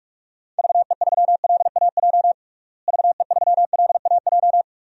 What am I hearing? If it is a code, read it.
Morse code, heard as a keyed tone, about 46 wpm, 700 Hz. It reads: VE3LAJ VE3LAJ
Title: DXCC-100 Call Signs - 6 Characters - Repeated 2x - 45wpm